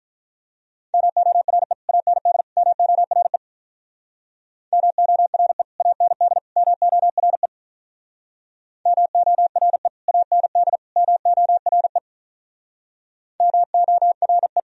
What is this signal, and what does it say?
Morse code, heard as a keyed tone, about 45 wpm, 700 Hz. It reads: MORE AND MORE MORE AND MORE MORE AND MORE MORE